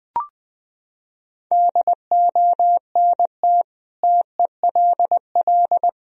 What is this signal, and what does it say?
Morse code, heard as a keyed tone, about 20 wpm, 700 Hz. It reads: DONT TELL
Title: Sentences from Top 1000 Words - Review Part 1 - 20wpm